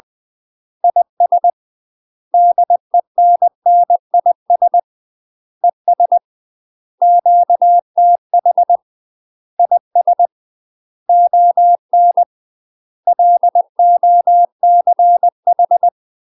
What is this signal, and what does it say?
Morse code, heard as a keyed tone, about 20 wpm, 700 Hz. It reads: IS DENNIS ES QTH IS ON LOCH
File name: UK MASTER QSOs TRAINING / 20WPM / UK MASTER QSOs 0073